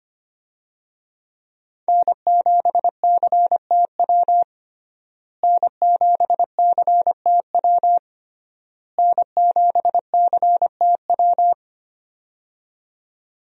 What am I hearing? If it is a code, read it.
Morse code, heard as a keyed tone, about 25 wpm, 700 Hz. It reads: N7CTW N7CTW N7CTW
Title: USA Call Signs - 5 Characters - Repeated 3x - 25wpm